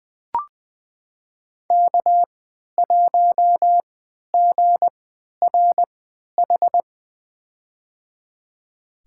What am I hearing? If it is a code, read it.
Morse code, heard as a keyed tone, about 20 wpm, spaced as for 11 wpm, 700 Hz. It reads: K1GRH
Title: USA Call Signs - 5 Characters - 3x Character Spacing - 20wpm